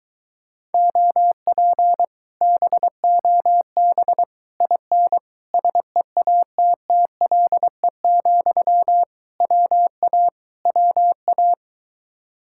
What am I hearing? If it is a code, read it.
Morse code, heard as a keyed tone, about 23 wpm, 700 Hz. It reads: OP BOB IN SEATTLE, WA WA